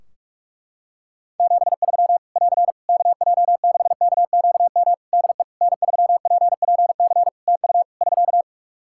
Morse code, heard as a keyed tone, about 46 wpm, 700 Hz. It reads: 73 <AR> KJ6KQM DE N3PPC TU <SK>
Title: K1USN SST QSO Practice 45wpm